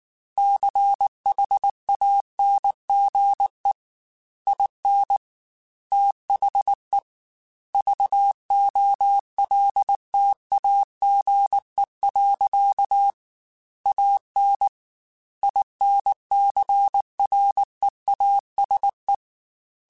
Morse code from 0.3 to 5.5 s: CHANGE IN

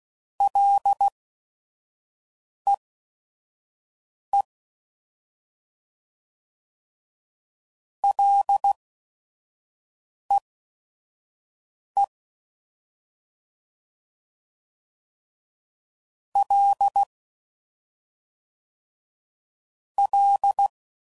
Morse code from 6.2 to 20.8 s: LEE L L

